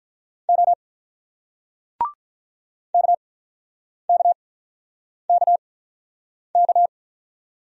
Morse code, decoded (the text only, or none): O KKKK